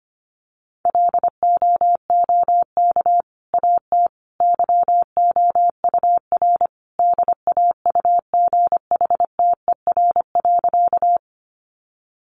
Morse code, heard as a keyed tone, about 25 wpm, 700 Hz. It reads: LOOK AT YOUR DAUGHTER.
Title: Sentences from Top 1500 Words 25wpm